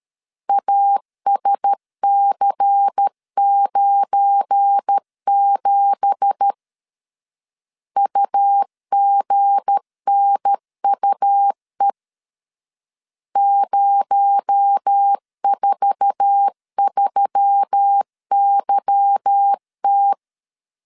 Morse code, heard as a keyed tone, about 13 wpm, 800 Hz. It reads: ASC97 UGNUE 043YT
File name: LEZIONI / 9 2867 12wpm 2322